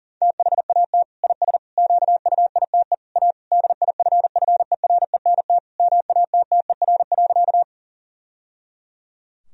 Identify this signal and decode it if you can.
Morse code, heard as a keyed tone, about 40 wpm, 700 Hz. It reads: THAT IS QUITE A DIFFERENT MATTER.
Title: Sentences from Top 1000 Words - Review Part 3 - 40wpm